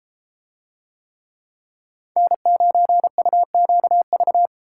Morse code, heard as a keyed tone, about 31 wpm, 700 Hz. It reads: N9UQV